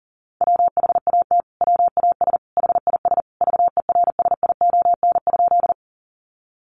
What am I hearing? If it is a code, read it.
Morse code, heard as a keyed tone, about 40 wpm, 700 Hz. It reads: WHAT WAS HIS VERSION?
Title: Sentences from Top 1400 Words 40wpm